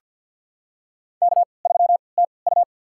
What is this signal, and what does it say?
Morse code, heard as a keyed tone, about 46 wpm, spaced as for 27 wpm, 690 Hz. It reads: K3TU
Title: USA Call Signs - 3x Character Spacing - 50wpm